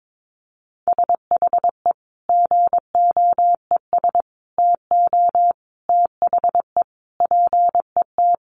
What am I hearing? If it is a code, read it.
Morse code, heard as a keyed tone, about 22 wpm, 700 Hz. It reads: SHE GOES TO THE PET
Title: Easy Phrases 22wpm